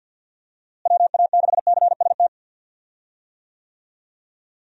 Morse code, heard as a keyed tone, about 53 wpm, 700 Hz. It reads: WA6CST
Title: USA Call Signs - 6 Characters - Repeated 3x - 50wpm